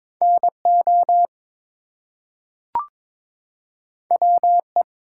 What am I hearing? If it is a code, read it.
Morse code, heard as a keyed tone, about 22 wpm, 700 Hz. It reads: NO WE
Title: Two Letter Words - 22wpm